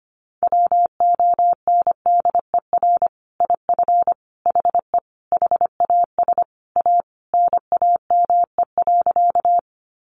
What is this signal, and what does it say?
Morse code, heard as a keyed tone, about 25 wpm, 700 Hz. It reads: WONDER IF HE HAS A NAME.
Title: Sentences from Top 500 Words 25wpm